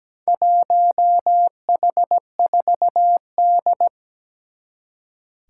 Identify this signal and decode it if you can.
Morse code, heard as a keyed tone, about 17 wpm, 700 Hz. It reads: 1H4D
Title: Four Challenging Letter-Number Combinations 17wpm